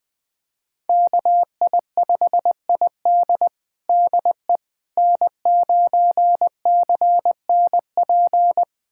Morse code, heard as a keyed tone, about 20 wpm, 700 Hz. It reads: KI5ID DE N9CNP
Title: USA Callsign DE Callsign - 5 Characters - 20wpm